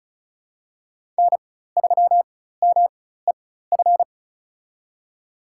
Morse code, heard as a keyed tone, about 34 wpm, spaced as for 16 wpm, 700 Hz. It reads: N3MEF
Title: USA Call Signs - 5 Characters - 4x Character Spacing - 35wpm